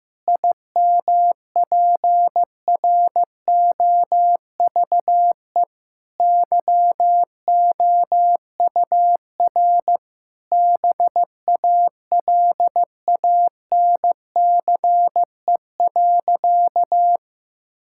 Morse code, heard as a keyed tone, about 15 wpm, 700 Hz. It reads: IMPROVE YOUR BALANCE.
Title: Sentences from Top 1000 Words - Review Part 6 - 15wpm